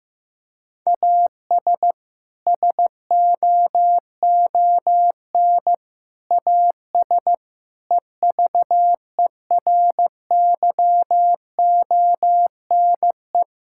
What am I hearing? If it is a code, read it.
Morse code, heard as a keyed tone, about 16 wpm, 700 Hz. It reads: AS SOON AS EVERYONE
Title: Sentences from Top 1000 Words - Review Part 5 - 15wpm